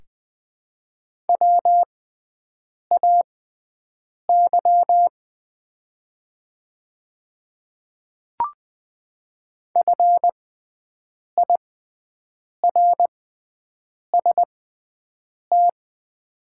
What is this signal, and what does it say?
Morse code, heard as a keyed tone, about 20 wpm, spaced as for 7 wpm, 700 Hz. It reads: WAY FIRST
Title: 6x Character Spacing - Top 100 Words - 20wpm